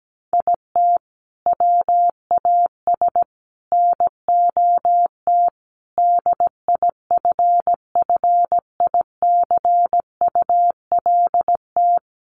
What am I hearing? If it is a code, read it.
Morse code, heard as a keyed tone, about 17 wpm, 700 Hz. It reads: IT WAS NOT DIFFICULT
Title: Sentences from Top 1500 Words 17wpm